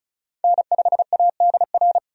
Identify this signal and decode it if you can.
Morse code, heard as a keyed tone, about 34 wpm, 700 Hz. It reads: N5ADR
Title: USA Call Signs - 5 Characters - 35wpm